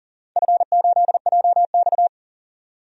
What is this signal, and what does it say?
Morse code, heard as a keyed tone, about 40 wpm, 700 Hz. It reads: F8JX
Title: DXCC-100 Call Signs - 4 Characters - 40wpm